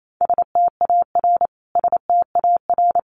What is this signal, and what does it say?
Morse code, heard as a keyed tone, about 28 wpm, 700 Hz. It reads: STAR STAR
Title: Top 500 Words Repeated 28wpm